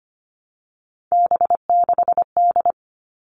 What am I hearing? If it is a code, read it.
Morse code, heard as a keyed tone, about 25 wpm, 700 Hz. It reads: B6D
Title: All Cut Numbers - 3-Digit 25wpm